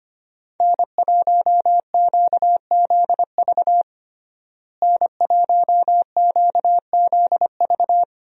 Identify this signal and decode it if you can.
Morse code, heard as a keyed tone, about 25 wpm, 700 Hz. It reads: N1QZV N1QZV
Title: USA Call Signs - 5 Characters - Repeated 2x - 25wpm